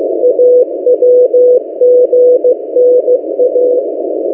Morse code, heard as a keyed tone, about 16 wpm, 500 Hz. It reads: AWGNA
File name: M08a- -1355utc-8096khz